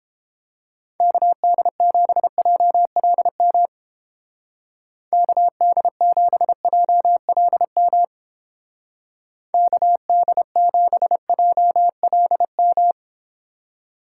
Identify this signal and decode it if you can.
Morse code, heard as a keyed tone, about 29 wpm, 700 Hz. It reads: KD7JLM KD7JLM KD7JLM